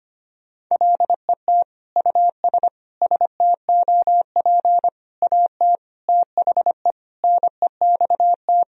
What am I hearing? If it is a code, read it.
Morse code, heard as a keyed tone, about 25 wpm, 700 Hz. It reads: LET US STOP AT THE NEXT
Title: Sentences from Top 400 Words and Abbreviations 25wpm